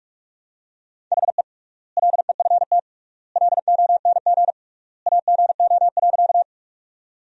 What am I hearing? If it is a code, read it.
Morse code, heard as a keyed tone, about 47 wpm, 700 Hz. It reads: HE LEFT LONG AGO.
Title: 3x Word-Spacing - Sentences from Top 500 Words - 45wpm